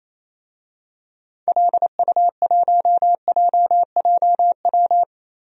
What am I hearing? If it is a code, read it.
Morse code, heard as a keyed tone, about 28 wpm, 700 Hz. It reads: LU1JJW
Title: DXCC-100 Call Signs - 6 Characters - Repeated 3x - 28wpm